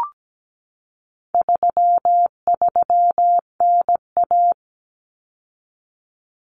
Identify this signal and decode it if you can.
Morse code, heard as a keyed tone, about 17 wpm, 700 Hz. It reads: 33NA
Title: Common Cut Numbers - 4-Digit 17wpm